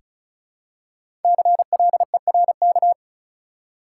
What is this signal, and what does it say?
Morse code, heard as a keyed tone, about 36 wpm, 700 Hz. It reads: CLERK
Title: Five Letter Words - 35wpm